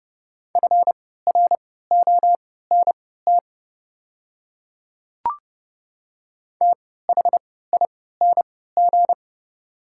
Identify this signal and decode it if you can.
Morse code, heard as a keyed tone, about 30 wpm, spaced as for 17 wpm, 690 Hz. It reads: FRONT THING
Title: 3x Character Spacing - Top 500 Words - 30wpm